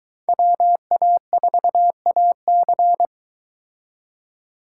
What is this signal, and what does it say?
Morse code, heard as a keyed tone, about 23 wpm, 700 Hz. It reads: WA4AC